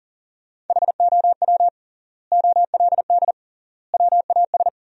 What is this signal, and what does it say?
Morse code, heard as a keyed tone, about 37 wpm, 700 Hz. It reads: HOW OLD WAS
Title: 3x Word-Spacing - Sentences from Top 200 Words - 40wpm